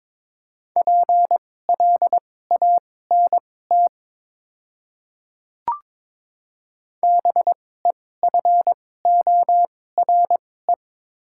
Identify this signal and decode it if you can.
Morse code, heard as a keyed tone, about 22 wpm, spaced as for 16 wpm, 700 Hz. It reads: PLANT BEFORE
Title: 2x Character Spacing - Top 300 Words - 22wpm